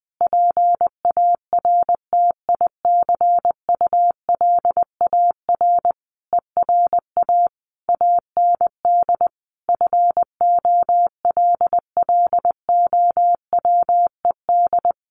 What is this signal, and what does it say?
Morse code, heard as a keyed tone, about 20 wpm, 700 Hz. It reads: PARTICULAR ERA AND FOLLOWED
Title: A Sound Of Thunder 20wpm